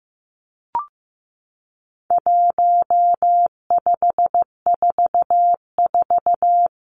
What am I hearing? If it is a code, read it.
Morse code, heard as a keyed tone, about 15 wpm, 700 Hz. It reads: 1544
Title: Numbers - 4-Digit 15wpm